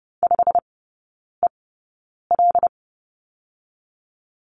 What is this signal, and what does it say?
Morse code, heard as a keyed tone, about 30 wpm, spaced as for 9 wpm, 700 Hz. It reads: 5EL
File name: Lesson 17e - Letter u - Three Characters 30wpm